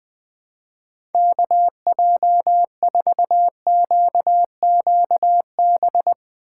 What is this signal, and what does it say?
Morse code, heard as a keyed tone, about 20 wpm, 700 Hz. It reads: KJ4QQB
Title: USA Call Signs - 6 Characters - 3x Character Spacing - 20wpm